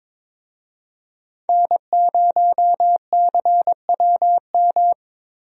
Morse code, heard as a keyed tone, about 22 wpm, 700 Hz. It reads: N0CWM
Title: USA Call Signs - 5 Characters - Repeated 3x - 22wpm